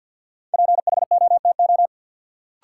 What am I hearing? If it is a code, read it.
Morse code, heard as a keyed tone, about 53 wpm, 700 Hz. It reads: PHOTO
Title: Sentences from Top 1200 Words - New Word Drill Alt - 50wpm